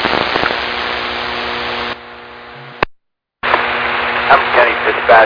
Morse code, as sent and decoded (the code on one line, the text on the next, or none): none